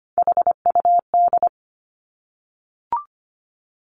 Morse code, heard as a keyed tone, about 25 wpm, 700 Hz. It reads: HUD
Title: Three Challenging Letter-Number Combinations 25wpm